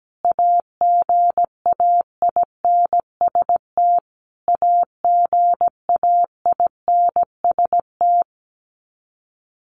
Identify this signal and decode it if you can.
Morse code, heard as a keyed tone, about 17 wpm, 700 Hz. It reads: AGAINST AGAINST
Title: Top 500 Words Repeated 17wpm